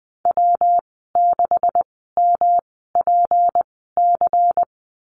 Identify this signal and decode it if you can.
Morse code, heard as a keyed tone, about 20 wpm, spaced as for 15 wpm, 700 Hz. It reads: W6MPC
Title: 161 Lesson 26g - Letter g - Call Signs 20wpm